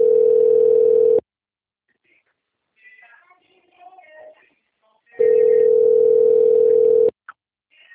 Ringback tone (North America), 440 + 480 Hz.
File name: out- - - - - .18123